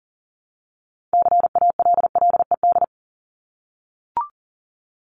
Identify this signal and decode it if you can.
Morse code, heard as a keyed tone, about 40 wpm, 700 Hz. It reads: CALLED
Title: ED Suffix - Top 100 Words with Base Words 40wpm